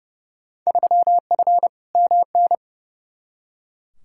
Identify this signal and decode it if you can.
Morse code, heard as a keyed tone, about 30 wpm, 700 Hz. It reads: 3F MN
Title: Field Day Exchanges Encore - 30wpm